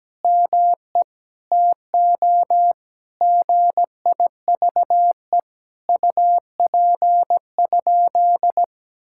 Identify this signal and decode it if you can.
Morse code, heard as a keyed tone, about 17 wpm, 700 Hz. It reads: ME TO GIVE UP?
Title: Sentences from Top 1600 Words 17wpm